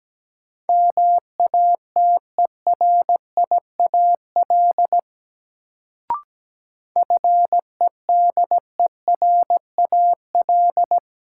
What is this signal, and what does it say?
Morse code, heard as a keyed tone, about 17 wpm, 700 Hz. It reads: MATERIAL FEDERAL
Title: Sentences from Top 1000 Words - Review Part 5 - New Word Drill - 17wpm